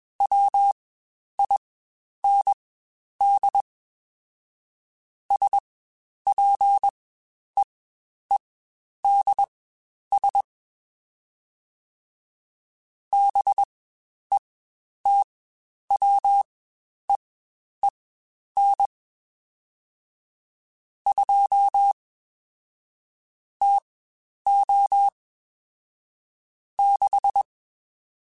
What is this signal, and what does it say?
Morse code, heard as a keyed tone, about 21 wpm, spaced as for 10 wpm, 800 Hz. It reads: WIND SPEEDS BETWEEN 2 TO 6
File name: Spaced Characters / Spaced Characters-mp3s / Weather 4 10 sp @20